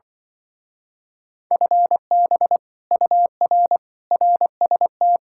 Morse code, heard as a keyed tone, about 25 wpm, 700 Hz. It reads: FB UR RST
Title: Speed-Racing Common QSO Phrases Encore - 20wpm